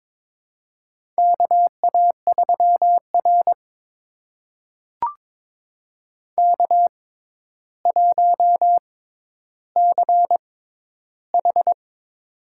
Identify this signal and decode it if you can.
Morse code, heard as a keyed tone, about 22 wpm, spaced as for 11 wpm, 700 Hz. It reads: KA3R K1CH
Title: USA Call Signs - 6x Character Spacing - 22wpm